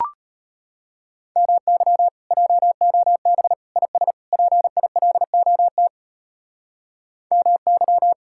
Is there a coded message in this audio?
Yes: Morse code from 1.4 to 8.1 s, heard as a keyed tone, about 32 wpm, 700 Hz. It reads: MY JOB IS PILOT MY